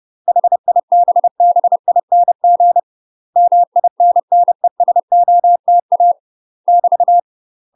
Morse code, heard as a keyed tone, about 30 wpm, 700 Hz. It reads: HIBBING MINNESOTA <BT>
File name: UK MASTER QSOs TRAINING / 30WPM / UK MASTER QSOs 0039